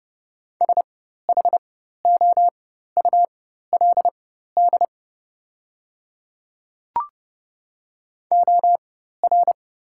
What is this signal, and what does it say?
Morse code, heard as a keyed tone, about 30 wpm, spaced as for 14 wpm, 700 Hz. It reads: SHOULD OR
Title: 4x Character Spacing - Top 300 Words - 30wpm